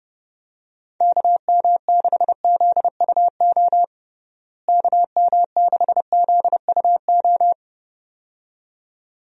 Morse code, heard as a keyed tone, about 30 wpm, 700 Hz. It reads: KM6ZUO KM6ZUO
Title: USA Call Signs - 6 Characters - Repeated 2x - 30wpm